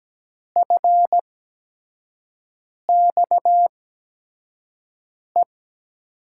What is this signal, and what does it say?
Morse code, heard as a keyed tone, about 17 wpm, spaced as for 5 wpm, 700 Hz. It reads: FXE
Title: USA Call Signs - 6 Characters - 8x Character Spacing - 17wpm